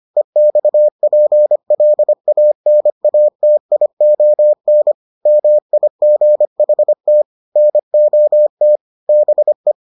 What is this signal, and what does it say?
Morse code, heard as a keyed tone, about 25 wpm, 600 Hz. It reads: EXPLANATION MIGHT NOT BE